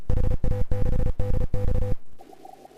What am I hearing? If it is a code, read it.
Morse code, heard as a keyed tone, about 33 wpm, 500 Hz, with noise. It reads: HA6DX